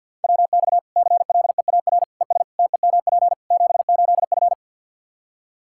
Morse code, heard as a keyed tone, about 52 wpm, 700 Hz. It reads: WX CLEAR ES TEMP 78F
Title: K1USN SST QSO Practice 50wpm